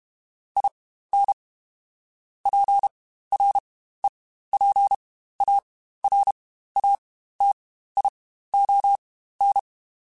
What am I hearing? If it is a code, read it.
Morse code, heard as a keyed tone, about 32 wpm, spaced as for 15 wpm, 800 Hz. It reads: IN PREPARATION
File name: Quotes Spaced / Quotes Spaced-mp3s / Persperation sp @30